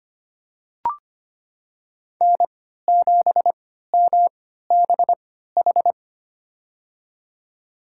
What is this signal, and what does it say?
Morse code, heard as a keyed tone, about 25 wpm, spaced as for 14 wpm, 700 Hz. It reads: N7MBH
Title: USA Call Signs - 5 Characters - 3x Character Spacing - 25wpm